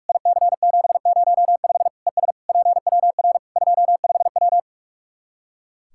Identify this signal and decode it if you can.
Morse code, heard as a keyed tone, about 46 wpm, 700 Hz. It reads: IC705 ES PWR 25W
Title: Practice QSO - Exchange 2 - Info - 45wpm